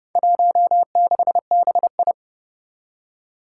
Morse code, heard as a keyed tone, about 30 wpm, 700 Hz. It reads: 16BI
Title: Four Challenging Letter-Number Combinations 30wpm